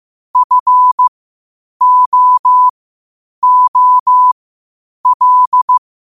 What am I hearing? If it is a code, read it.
Morse code, heard as a keyed tone, about 15 wpm, spaced as for 8 wpm, 1000 Hz. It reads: FOOL
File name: qotdfiles / 8wpm